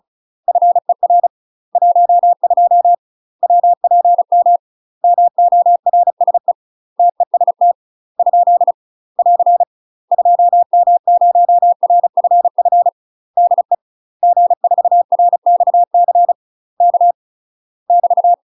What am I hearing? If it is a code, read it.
Morse code, heard as a keyed tone, about 34 wpm, 700 Hz. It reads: FER 12 WPM MORSE TEST ? <AR> 2M0RFF DE G4RXC K <BT>
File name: UK MASTER QSOs TRAINING/35WPM